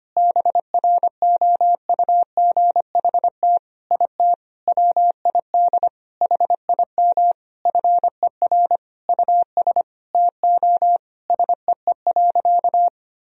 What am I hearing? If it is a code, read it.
Morse code, heard as a keyed tone, about 25 wpm, 700 Hz. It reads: BROUGHT IT WID HIM FER US TO SEE.
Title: Sentences from Top 500 Words and Abbreviations 25wpm